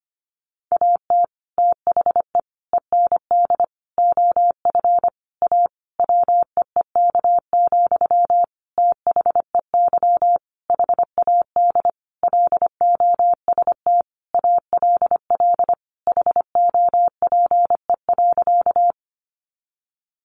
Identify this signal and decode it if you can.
Morse code, heard as a keyed tone, about 25 wpm, 700 Hz. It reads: AT THE END OF A WEEK, THEY HAD LOST ALL HOPE.
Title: Sentences from Top 1400 Words 25wpm